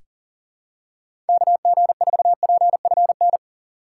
Morse code, heard as a keyed tone, about 40 wpm, 700 Hz. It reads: KG4PFN
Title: USA Call Signs - 6 Characters - 4x Character Spacing - 40wpm